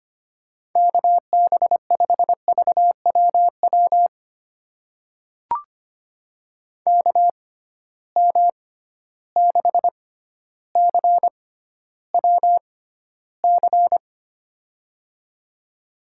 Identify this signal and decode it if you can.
Morse code, heard as a keyed tone, about 25 wpm, spaced as for 13 wpm, 700 Hz. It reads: KB5VWW KM6CWC
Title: USA Call Signs - 6 Characters - 6x Character Spacing - 25wpm